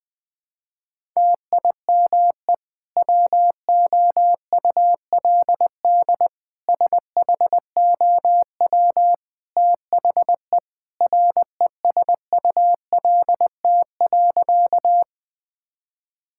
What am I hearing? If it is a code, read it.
Morse code, heard as a keyed tone, about 20 wpm, 700 Hz. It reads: TIME WOULD SHOW THE RESULT.
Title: Sentences from Top 1000 Words - Review Part 5 - 20wpm